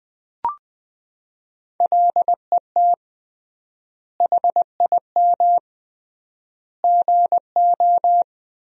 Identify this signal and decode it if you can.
Morse code, heard as a keyed tone, about 20 wpm, 700 Hz. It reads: LET HIM GO
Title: 3x Word-Spacing - Sentences from Top 500 Words - 20wpm